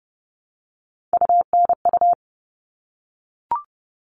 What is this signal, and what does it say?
Morse code, heard as a keyed tone, about 30 wpm, 700 Hz. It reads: UNU
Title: All Cut Numbers - 3-Digit 30wpm